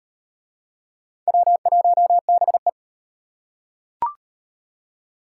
Morse code, heard as a keyed tone, about 38 wpm, 700 Hz. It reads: W1BE